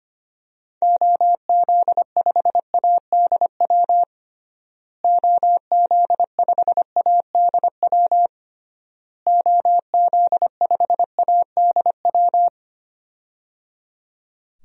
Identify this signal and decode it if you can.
Morse code, heard as a keyed tone, about 25 wpm, 700 Hz. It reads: OZ5ADW OZ5ADW OZ5ADW
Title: DXCC-100 Call Signs - 6 Characters - Repeated 3x - 25wpm